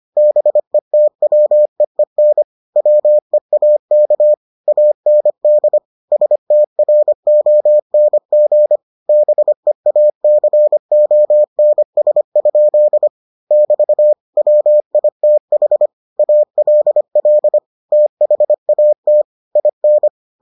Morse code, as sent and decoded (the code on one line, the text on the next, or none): -... - .-- . -. / .-- . .- -.- / .- -. -.. / ... - .-. --- -. --. / -... .- -.-. --- -. ... ..--.. / -...- / .-- .. - .... / .- .-.. .-.. / - .... .- - / .. -.
BETWEEN WEAK AND STRONG BEACONS? <BT> WITH ALL THAT IN